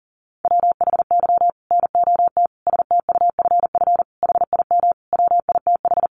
Morse code, heard as a keyed tone, about 40 wpm, 700 Hz. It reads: WHY NOT STUFF HIM WITH
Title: Sentences from Top 1400 Words 40wpm